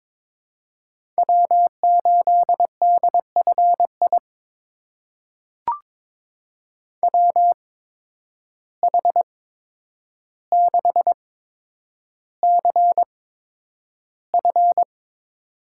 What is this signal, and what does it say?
Morse code, heard as a keyed tone, about 22 wpm, spaced as for 9 wpm, 700 Hz. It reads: W8DFI WH6CF
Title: USA Call Signs - 8x Character Spacing - 22wpm